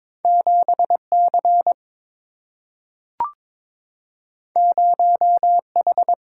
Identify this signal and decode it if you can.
Morse code, heard as a keyed tone, about 22 wpm, 690 Hz. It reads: 7C 0H
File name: practice / 22wpm / Two Letter-Number Combinations - 22wpm